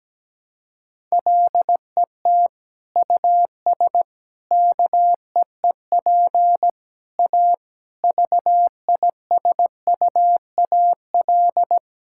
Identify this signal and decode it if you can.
Morse code, heard as a keyed tone, about 17 wpm, 700 Hz. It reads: LET US KEEP A VISUAL
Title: Sentences from Top 1000 Words - Review Part 6 - 17wpm